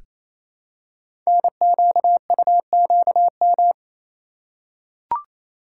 Morse code, heard as a keyed tone, about 27 wpm, 700 Hz. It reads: NQUQM
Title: Code Groups - 5 Letters - 28wpm